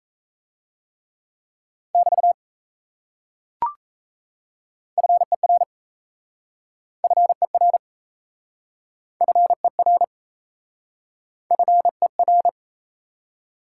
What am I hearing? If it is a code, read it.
Morse code, heard as a keyed tone, about 37 wpm, 700 Hz. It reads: <BT> FER FER FER FER